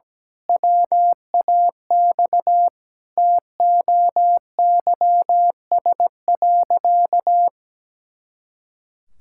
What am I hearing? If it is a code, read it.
Morse code, heard as a keyed tone, about 17 wpm, 700 Hz. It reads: WAX TOYS.
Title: Pangram Challenge v2 - 17wpm